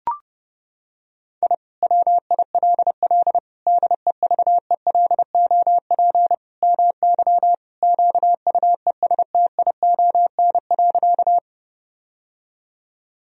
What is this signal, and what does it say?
Morse code, heard as a keyed tone, about 32 wpm, 700 Hz. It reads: I WILL DEVELOP MY QUESTION.
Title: Sentences from Top 1000 Words - Review Part 2 - 30wpm